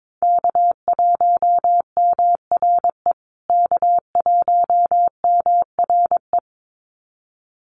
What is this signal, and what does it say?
Morse code, heard as a keyed tone, about 22 wpm, 700 Hz. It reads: K1MRE K1MRE
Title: USA Call Signs Repeated 2x 22wpm